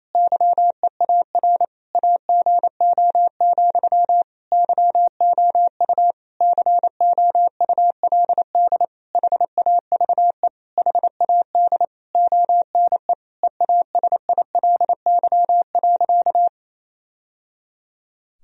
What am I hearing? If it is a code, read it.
Morse code, heard as a keyed tone, about 28 wpm, 700 Hz. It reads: YEAR AGO, YOU COULD HAVE HAD ONE EASILY.
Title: Sentences from Top 1000 Words - Review Part 3 - 28wpm